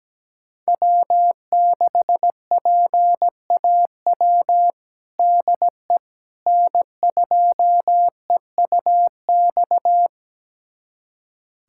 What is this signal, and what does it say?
Morse code, heard as a keyed tone, about 17 wpm, 700 Hz. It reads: W6PAW DE N2EUX
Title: USA Callsign DE Callsign - 5 Characters - 17wpm